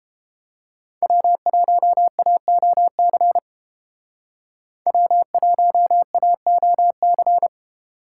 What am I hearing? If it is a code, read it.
Morse code, heard as a keyed tone, about 32 wpm, 700 Hz. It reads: W1AOC W1AOC